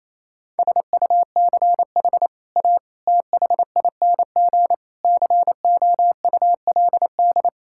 Morse code, heard as a keyed tone, about 28 wpm, 700 Hz. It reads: SUCH A THING COULD
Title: Sentences from Top 1000 Words - Review Part 6 - 28wpm